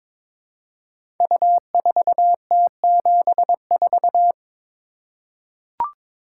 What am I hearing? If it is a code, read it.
Morse code, heard as a keyed tone, about 22 wpm, 690 Hz. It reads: U4T74
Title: Code Groups - 5 Letter:Numbers - 22wpm